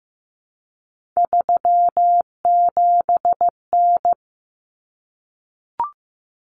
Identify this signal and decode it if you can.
Morse code, heard as a keyed tone, about 15 wpm, 700 Hz. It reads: 37N
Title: Common Cut Numbers - 3-Digit 15wpm